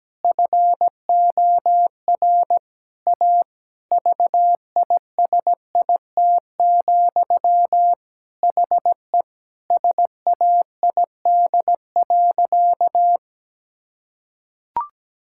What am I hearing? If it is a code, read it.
Morse code, heard as a keyed tone, about 17 wpm, 700 Hz. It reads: FOR A VISIT, HE SAID.
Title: Sentences from Top 1000 Words - Review Part 6 - 17wpm